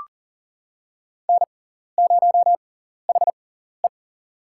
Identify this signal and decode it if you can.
Morse code, heard as a keyed tone, about 40 wpm, spaced as for 14 wpm, 700 Hz. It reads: N0HE
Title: USA Call Signs - 4 Characters - 6x Character Spacing - 40wpm